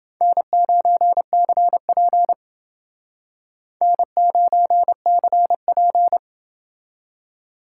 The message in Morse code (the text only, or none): N9CP N9CP